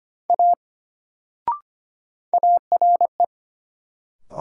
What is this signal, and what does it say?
Morse code, heard as a keyed tone, about 25 wpm, 700 Hz. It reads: A ARE
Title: Sentences from Top 100 Words - New Word Drill - 25wpm